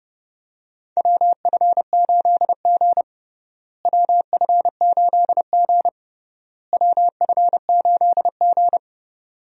Morse code, heard as a keyed tone, about 30 wpm, 700 Hz. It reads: WF8G WF8G WF8G
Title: USA Call Signs - 4 Characters - Repeated 3x - 30wpm